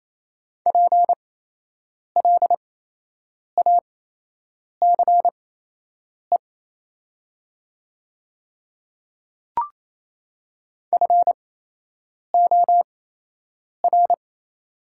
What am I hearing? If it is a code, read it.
Morse code, heard as a keyed tone, about 28 wpm, spaced as for 8 wpm, 700 Hz. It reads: PLACE FOR
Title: 8x Character Spacing - Top 200 Words - 28wpm